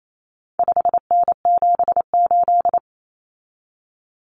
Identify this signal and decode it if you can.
Morse code, heard as a keyed tone, about 28 wpm, 700 Hz. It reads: 5N78
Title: Common Cut Numbers - 4-Digit 28wpm